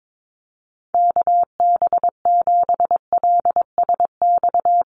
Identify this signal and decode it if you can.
Morse code, heard as a keyed tone, about 22 wpm, 700 Hz. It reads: KB7LSX
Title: USA Call Signs Repeated 2x 22wpm